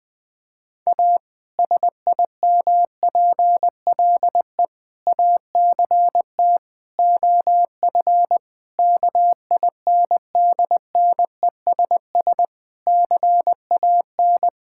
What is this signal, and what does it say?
Morse code, heard as a keyed tone, about 20 wpm, 700 Hz. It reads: A SIMPLE ACT OF KINDNESS CAN
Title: Difficult Sentences with Hints - 20wpm